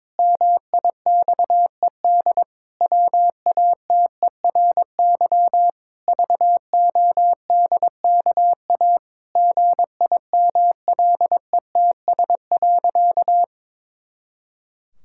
Morse code, heard as a keyed tone, about 21 wpm, 700 Hz. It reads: MIXED WATERY VODKA GIMLETS.
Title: Pangram Challenge v2 - 22wpm